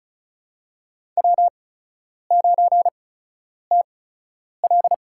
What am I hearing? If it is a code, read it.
Morse code, heard as a keyed tone, about 36 wpm, spaced as for 10 wpm, 700 Hz. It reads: W9TL
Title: USA Call Signs - 4 Characters - 8x Character Spacing - 35wpm